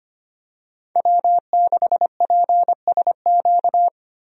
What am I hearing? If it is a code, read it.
Morse code, heard as a keyed tone, about 25 wpm, 700 Hz. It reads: W6PSQ
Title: USA Call Signs - 5 Characters - Repeated 3x - 25wpm